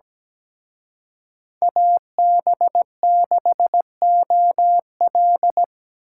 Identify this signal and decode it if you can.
Morse code, heard as a keyed tone, about 20 wpm, 700 Hz. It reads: AB6OL
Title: Speed Racing - USA 2x2 Call Signs 17wpm